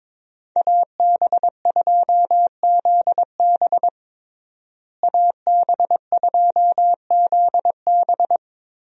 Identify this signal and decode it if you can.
Morse code, heard as a keyed tone, about 22 wpm, 700 Hz. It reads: AB2ZB AB2ZB
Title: USA Call Signs - 5 Characters - Repeated 3x - 22wpm